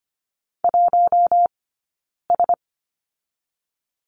Morse code, heard as a keyed tone, about 25 wpm, 700 Hz. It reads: 1 S
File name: Lesson 15d - Number 5 - Two Characters 25wpm